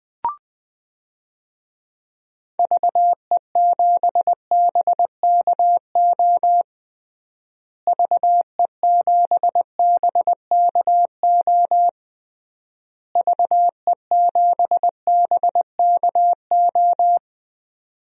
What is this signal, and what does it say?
Morse code, heard as a keyed tone, about 21 wpm, 700 Hz. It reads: VE7BKO VE7BKO VE7BKO
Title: DXCC-100 Call Signs - 6 Characters - Repeated 3x - 20wpm